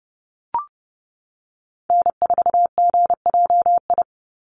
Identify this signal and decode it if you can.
Morse code, heard as a keyed tone, about 30 wpm, 700 Hz. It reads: N4GJI
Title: USA Call Signs 30wpm